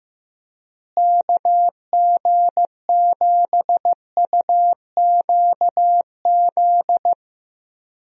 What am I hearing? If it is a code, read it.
Morse code, heard as a keyed tone, about 15 wpm, 700 Hz. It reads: KG7UQZ
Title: USA Call Signs - 6 Characters - Repeated 2x - 15wpm